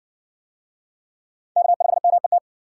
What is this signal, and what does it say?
Morse code, heard as a keyed tone, about 57 wpm, 700 Hz. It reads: K5MET